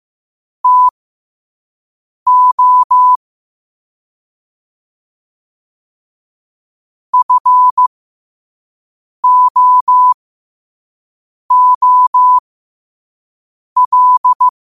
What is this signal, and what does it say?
Morse code, heard as a keyed tone, about 15 wpm, spaced as for 5 wpm, 1000 Hz. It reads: TO FOOL
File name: qotdfiles / 5wpm